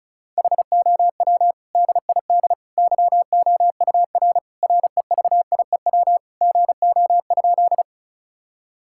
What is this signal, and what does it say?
Morse code, heard as a keyed tone, about 36 wpm, 700 Hz. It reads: HOW DID YOUR REVIEW GO?
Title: Sentences from Top 1000 Words - Review Part 6 - 35wpm